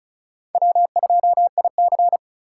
Morse code, heard as a keyed tone, about 45 wpm, 700 Hz. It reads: W2IC